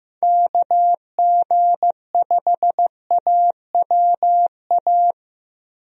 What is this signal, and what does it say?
Morse code, heard as a keyed tone, about 15 wpm, 700 Hz. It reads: KG5AWA
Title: USA Call Signs - 6 Characters - Repeated 2x - 15wpm